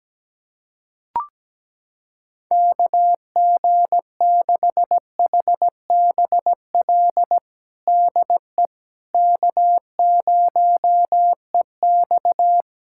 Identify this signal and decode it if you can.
Morse code, heard as a keyed tone, about 17 wpm, 700 Hz. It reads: KG6HBL DE K0EX
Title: K1USN SST QSO Practice 17wpm